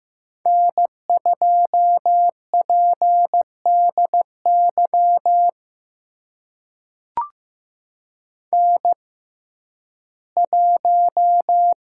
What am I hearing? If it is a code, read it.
Morse code, heard as a keyed tone, about 15 wpm, spaced as for 11 wpm, 700 Hz. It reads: N2PDY N1
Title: USA Call Signs - 5 Characters - 6x Character Spacing - 15wpm